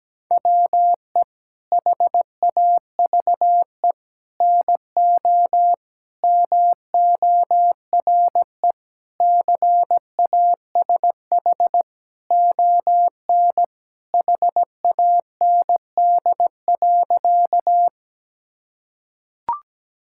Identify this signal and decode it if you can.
Morse code, heard as a keyed tone, about 17 wpm, 700 Hz. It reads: WE HAVE NO MORE CASH ON HAND.
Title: Sentences from Top 1000 Words - Review Part 5 - 17wpm